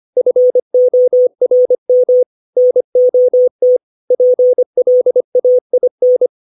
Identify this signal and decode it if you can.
Morse code, heard as a keyed tone, about 25 wpm, 500 Hz. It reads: FORM NOT PLAIN
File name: 632 words / 632 Words 25WPM